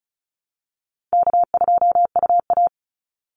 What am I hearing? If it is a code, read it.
Morse code, heard as a keyed tone, about 34 wpm, 700 Hz. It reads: K2UA
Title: Easy Call Signs Repeated 3x - 35wpm